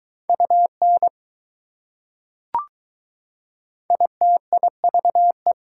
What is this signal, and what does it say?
Morse code, heard as a keyed tone, about 20 wpm, 700 Hz. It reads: UN ITIVE